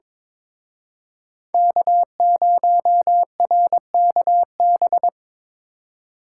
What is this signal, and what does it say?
Morse code, heard as a keyed tone, about 25 wpm, 700 Hz. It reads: K0RKB
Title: Speed Racing - USA 1x3 Call Signs 22wpm